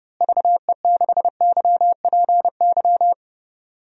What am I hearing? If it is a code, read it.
Morse code, heard as a keyed tone, about 32 wpm, 700 Hz. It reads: VE6YPY
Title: DXCC-100 Call Signs - 6 Characters - Repeated 2x - 30wpm